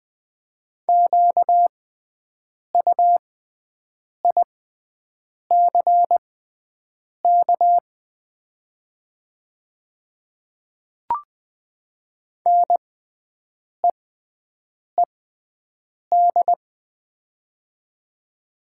Morse code, heard as a keyed tone, about 20 wpm, spaced as for 7 wpm, 700 Hz. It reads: QUICK NEED
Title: 6x Character Spacing - Top 500 Words - 20wpm